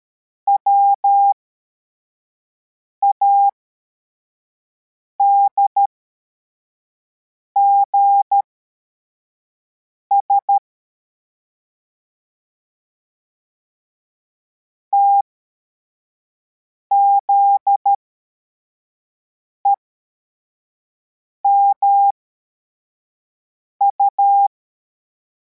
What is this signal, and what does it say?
Morse code, heard as a keyed tone, about 13 wpm, spaced as for 4 wpm, 800 Hz. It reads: WADGS TZEMU
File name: formation / cw / Text62